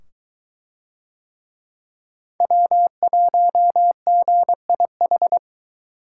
Morse code, heard as a keyed tone, about 21 wpm, 700 Hz. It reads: W1GIH